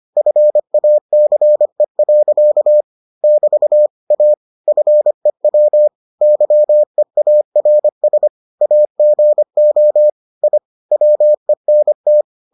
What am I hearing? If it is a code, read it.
Morse code, heard as a keyed tone, about 25 wpm, 600 Hz. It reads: FACE. <BT> A FEW YEARS AGO I WENT